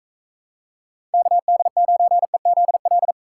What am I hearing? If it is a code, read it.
Morse code, heard as a keyed tone, about 45 wpm, 700 Hz. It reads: KD9EZL